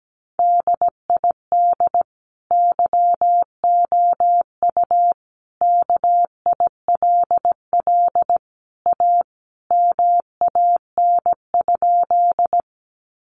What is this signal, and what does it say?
Morse code, heard as a keyed tone, about 17 wpm, 700 Hz. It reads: DID YOU KILL A MAN?
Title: Sentences from Top 1400 Words 17wpm